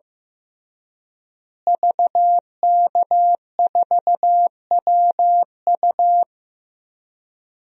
Morse code, heard as a keyed tone, about 15 wpm, 700 Hz. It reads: VK4WU
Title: DXCC-100 Call Signs - 5 Characters - Repeated 3x - 15wpm